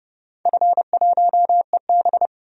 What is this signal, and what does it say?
Morse code, heard as a keyed tone, about 32 wpm, 700 Hz. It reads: F1EB
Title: DXCC-100 Call Signs - 4 Characters - 30wpm